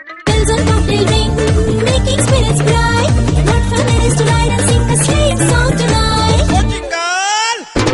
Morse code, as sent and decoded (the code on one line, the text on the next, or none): none